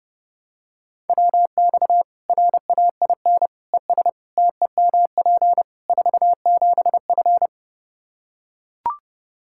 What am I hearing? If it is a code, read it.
Morse code, heard as a keyed tone, about 30 wpm, 700 Hz. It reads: WX RAIN ES TEMP 47F
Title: Practice QSO - Exchange 2 - Info - 30wpm